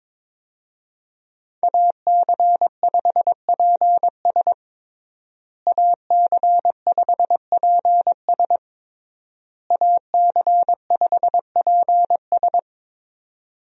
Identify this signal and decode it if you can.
Morse code, heard as a keyed tone, about 21 wpm, 700 Hz. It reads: AC5PS AC5PS AC5PS
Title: USA Call Signs - 5 Characters - Repeated 3x - 22wpm